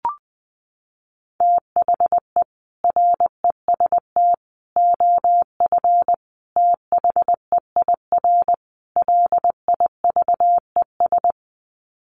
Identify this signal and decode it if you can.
Morse code, heard as a keyed tone, about 20 wpm, 700 Hz. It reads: THE REST OF THEIR LIVES
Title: Sets of 5 Words Encore 20wpm